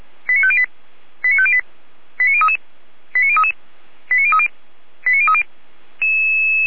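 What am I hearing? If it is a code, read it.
Five-tone selective call (ZVEI), heard as quick runs of five tones, tones of about 70 ms. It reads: 89508 89508 89033 89033 89030 89030
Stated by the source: ZVEI five-tone selective call received by SDR, from the Feuerwehr Orlamünde (fire brigade) audio server.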